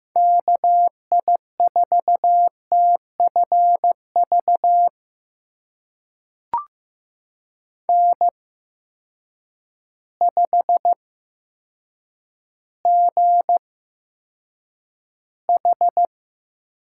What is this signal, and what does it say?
Morse code, heard as a keyed tone, about 15 wpm, spaced as for 8 wpm, 700 Hz. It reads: KI4TFV N5GH
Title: USA Call Signs - 8x Character Spacing - 15wpm